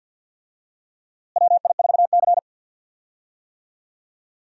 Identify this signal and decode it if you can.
Morse code, heard as a keyed tone, about 53 wpm, 700 Hz. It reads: WI4C
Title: USA Call Signs - 4 Characters - Repeated 3x - 50wpm